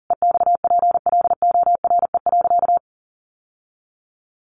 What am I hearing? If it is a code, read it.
Morse code, heard as a keyed tone, about 40 wpm, 700 Hz. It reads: EXPLORE.
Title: Sentences from Top 1100 Words 40wpm